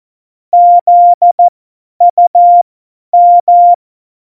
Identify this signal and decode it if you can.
Morse code, heard as a keyed tone, about 14 wpm, spaced as for 10 wpm, 700 Hz. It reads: ZUM
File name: Morsekurs / Witze / Witz 17.Dezember 2019, 50 F70